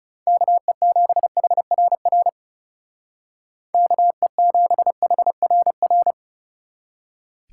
Morse code, heard as a keyed tone, about 37 wpm, 700 Hz. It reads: KE7HRR KE7HRR
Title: Speed Racing - USA 2x3 Call Signs 30wpm